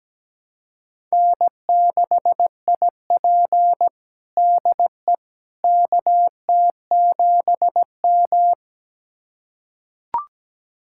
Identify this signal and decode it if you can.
Morse code, heard as a keyed tone, about 17 wpm, 700 Hz. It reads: N6IP DE KT7M
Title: USA Callsign DE Callsign - 4 Characters - 17wpm